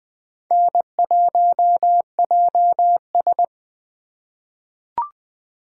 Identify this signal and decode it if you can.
Morse code, heard as a keyed tone, about 20 wpm, 700 Hz. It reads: N1JS
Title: USA Call Signs - 4 Characters - 4x Character Spacing - 20wpm